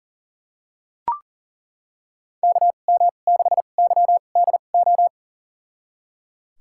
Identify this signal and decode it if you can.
Morse code, heard as a keyed tone, about 38 wpm, spaced as for 28 wpm, 700 Hz. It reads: KM6YDO
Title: USA Call Signs - 6 Characters - 2x Character Spacing - 40wpm